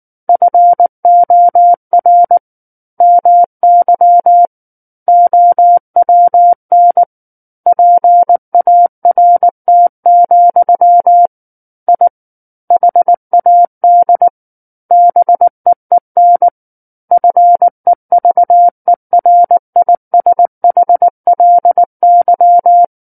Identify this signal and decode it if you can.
Morse code, heard as a keyed tone, about 19 wpm, 700 Hz. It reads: FOR MY OWN PART, I HAD BEEN FEVERISHLY
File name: Book One / 10-Chapter 10 - 19 WPM